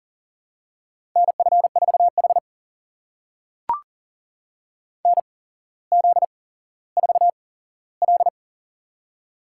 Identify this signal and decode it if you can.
Morse code, heard as a keyed tone, about 39 wpm, spaced as for 17 wpm, 700 Hz. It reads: NF4H NZ4L
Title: USA Call Signs - 4 Characters - 8x Character Spacing - 40wpm